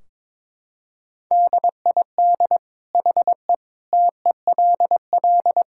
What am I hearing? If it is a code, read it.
Morse code, heard as a keyed tone, about 22 wpm, 700 Hz. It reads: DID HE TELL
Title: Sentences from Top 1600 Words 22wpm